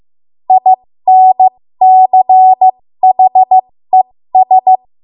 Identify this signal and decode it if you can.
Morse code, heard as a keyed tone, about 15 wpm, spaced as for 13 wpm, 750 Hz. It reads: INCHES
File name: Morse / 13WPM